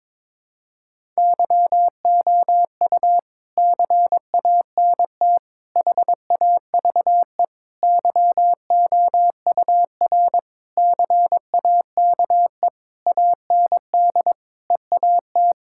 Morse code, heard as a keyed tone, about 22 wpm, 700 Hz. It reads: YOU CANT HAVE YOUR CAKE AND EAT
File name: Sentences with 9-Word Phrases - 22wpm